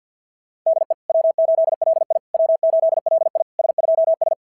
Morse code, heard as a keyed tone, about 48 wpm, 690 Hz. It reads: DE W8LI W8LI S2S
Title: SOTA QSO Practice 50wpm